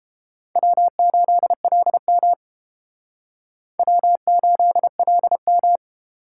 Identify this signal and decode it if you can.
Morse code, heard as a keyed tone, about 30 wpm, 700 Hz. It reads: W8LM W8LM